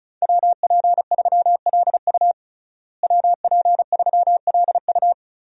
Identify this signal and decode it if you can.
Morse code, heard as a keyed tone, about 34 wpm, 700 Hz. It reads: WP3LU WP3LU
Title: USA Call Signs - 5 Characters - Repeated 2x - 35wpm